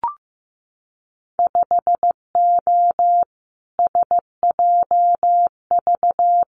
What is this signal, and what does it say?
Morse code, heard as a keyed tone, about 15 wpm, 700 Hz. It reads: 5O SJV
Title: Winter Field Day Exchanges Encore 15wpm